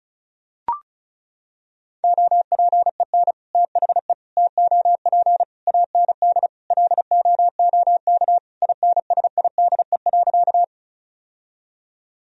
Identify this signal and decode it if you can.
Morse code, heard as a keyed tone, about 34 wpm, 700 Hz. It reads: OPEN THE TOP AND LOOK INSIDE.
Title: Sentences from Top 1000 Words - Review Part 2 - 35wpm